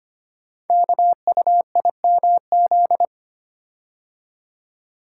Morse code, heard as a keyed tone, about 25 wpm, 700 Hz. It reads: KUIMZ
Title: Code Groups - 5 Letters - 25wpm